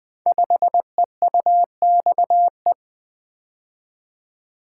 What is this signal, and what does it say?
Morse code, heard as a keyed tone, about 20 wpm, 700 Hz. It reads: 5EUXE
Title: Code Groups - 5 Letter:Numbers - 20wpm